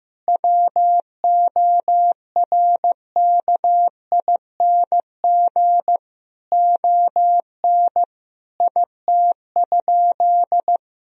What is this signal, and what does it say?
Morse code, heard as a keyed tone, about 15 wpm, 700 Hz. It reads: WORKING ON IT?
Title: Sentences from Top 1000 Words - Review Part 4 - 15wpm